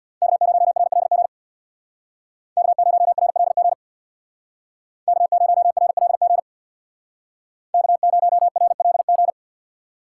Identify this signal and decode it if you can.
Morse code, heard as a keyed tone, about 61 wpm, 700 Hz. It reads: K0RLG K0RLG K0RLG K0RLG
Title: Speed Racing - USA 1x3 Call Signs 50wpm